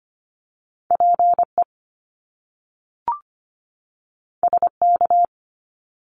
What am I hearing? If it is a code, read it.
Morse code, heard as a keyed tone, about 25 wpm, 700 Hz. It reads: PE SK
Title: Canadian Provinces and Territories 25wpm